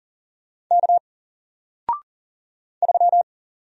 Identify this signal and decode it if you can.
Morse code, heard as a keyed tone, about 40 wpm, 700 Hz. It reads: K 3
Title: Single Letter-Number Alt 40wpm